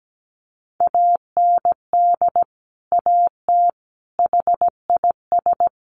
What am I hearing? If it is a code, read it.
Morse code, heard as a keyed tone, about 17 wpm, 700 Hz. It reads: AND AT HIS
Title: Sentences from Top 1500 Words 17wpm